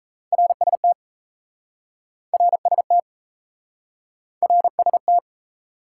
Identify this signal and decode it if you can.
Morse code, heard as a keyed tone, about 39 wpm, 700 Hz. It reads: RST RST RST